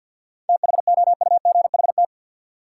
Morse code, heard as a keyed tone, about 53 wpm, 700 Hz. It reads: THOUGHT
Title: Sets of 2 Words - New Word Drill - 50wpm